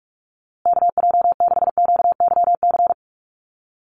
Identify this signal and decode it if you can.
Morse code, heard as a keyed tone, about 44 wpm, 700 Hz. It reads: KJ6QYC
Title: USA Call Signs Repeated 2x 45wpm